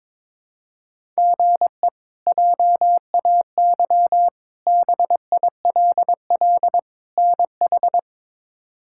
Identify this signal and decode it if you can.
Morse code, heard as a keyed tone, about 22 wpm, 700 Hz. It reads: GE JAY BILL NH
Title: K1USN SST QSO Practice - 22wpm